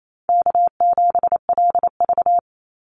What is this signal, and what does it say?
Morse code, heard as a keyed tone, about 28 wpm, 700 Hz. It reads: K7LV
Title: Call Signs - 28wpm